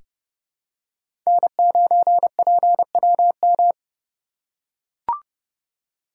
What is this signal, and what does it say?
Morse code, heard as a keyed tone, about 30 wpm, 700 Hz. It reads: N9PWM
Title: USA Call Signs - 2x Character Spacing - 30wpm